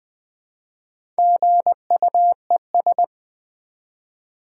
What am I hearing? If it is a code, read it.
Morse code, heard as a keyed tone, about 20 wpm, 700 Hz. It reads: GUES
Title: Build A Word - 1 Letter At A Time - 5-Letter Words - 20wpm